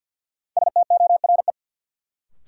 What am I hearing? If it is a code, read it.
Morse code, heard as a keyed tone, about 49 wpm, 700 Hz. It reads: STORE
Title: Sentences from Top 1000 Words - Review Part 5 - New Word Drill - 50wpm